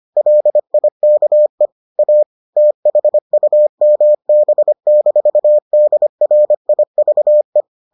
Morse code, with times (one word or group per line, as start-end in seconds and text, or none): 0.2-1.7 LIKE
2.0-2.2 A
2.6-7.6 THUMB-DRIVE